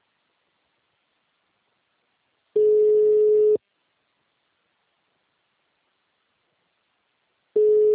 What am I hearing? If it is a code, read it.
Ringback tone (Europe, 425 Hz).